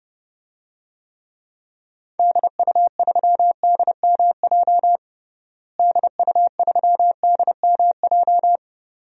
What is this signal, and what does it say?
Morse code, heard as a keyed tone, about 29 wpm, 700 Hz. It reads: DU3DMJ DU3DMJ
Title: DXCC-100 Call Signs - 6 Characters - Repeated 2x - 30wpm